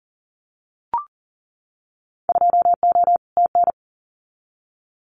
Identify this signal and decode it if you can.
Morse code, heard as a keyed tone, about 40 wpm, 700 Hz. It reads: 2O TN
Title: Winter Field Day Exchanges 40wpm